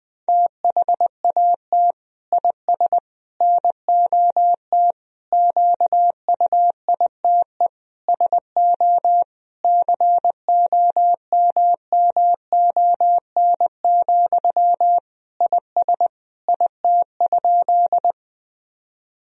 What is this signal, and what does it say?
Morse code, heard as a keyed tone, about 20 wpm, 700 Hz. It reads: THAT IS NOT QUITE SO COMMON, IS IT?
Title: Sentences from Top 1000 Words - Review Part 3 - 20wpm